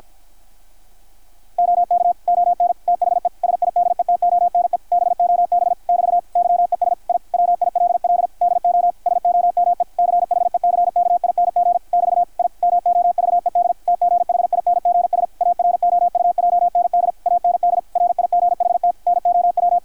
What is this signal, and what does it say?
Morse code, heard as a keyed tone, about 53 wpm, 720 Hz, with radio noise. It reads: OK ON THE SIDETONE BOB <BT> YES I WILL DO SOME CHECKING <BT> I MOVED TGHINGS AROUJND AND RIGHT NOW
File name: archived / CRSnet